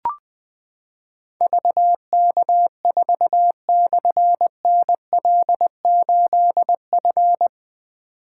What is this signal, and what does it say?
Morse code, heard as a keyed tone, about 20 wpm, 700 Hz. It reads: VK4/NL8F
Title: Difficult Call Signs 20wpm